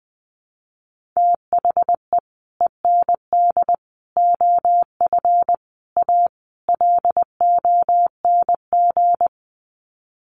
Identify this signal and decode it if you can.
Morse code, heard as a keyed tone, about 20 wpm, 700 Hz. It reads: THE END OF A LONG
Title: Sets of 5 Words Encore2 20wpm